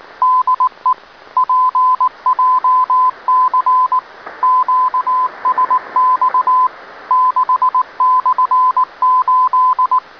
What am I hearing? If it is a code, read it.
Morse code, heard as a keyed tone, about 19 wpm, 1020 Hz. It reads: DE PJC QSX 6/8